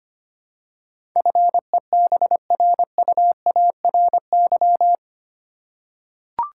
Morse code, heard as a keyed tone, about 25 wpm, 700 Hz. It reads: FEBRUARY
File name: Eight Letter Words - 25wpm